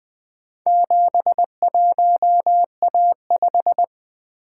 Morse code, heard as a keyed tone, about 20 wpm, 700 Hz. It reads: 71A5
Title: Four Letter-Number Combinations - 20wpm